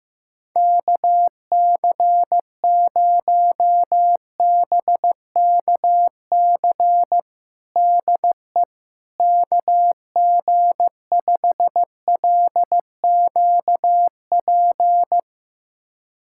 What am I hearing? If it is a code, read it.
Morse code, heard as a keyed tone, about 15 wpm, 700 Hz. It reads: KC0BKC DE KG5LQP
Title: USA Callsign DE Callsign - 6 Characters - 15wpm